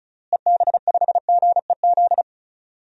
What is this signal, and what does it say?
Morse code, heard as a keyed tone, about 36 wpm, 700 Hz. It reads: EB5GEZ
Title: DXCC-100 Call Signs - 6 Characters - Repeated 2x - 35wpm